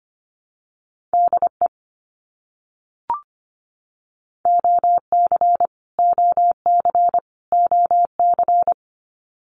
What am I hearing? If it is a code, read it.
Morse code, heard as a keyed tone, about 25 wpm, 700 Hz. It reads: DE OC OC OC
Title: Abbreviations Repeated 3x - 25wpm